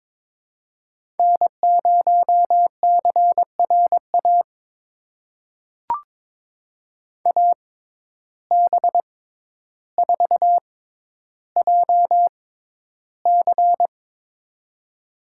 Morse code, heard as a keyed tone, about 22 wpm, spaced as for 11 wpm, 700 Hz. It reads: N0CRA AB4JC
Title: USA Call Signs - 5 Characters - 6x Character Spacing - 22wpm